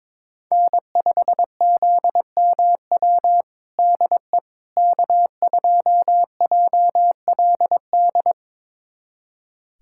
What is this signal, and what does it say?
Morse code, heard as a keyed tone, about 22 wpm, 700 Hz. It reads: N5ZMW DE K2JLD
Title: USA Callsign DE Callsign - 5 Characters - 22wpm